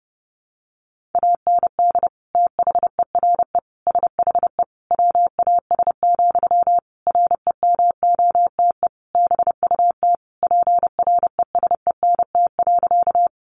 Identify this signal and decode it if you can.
Morse code, heard as a keyed tone, about 30 wpm, 700 Hz. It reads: AND THERE SHE WAS, REMOTE BUT PRESENT.
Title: Sentences from Top 1300 Words 30wpm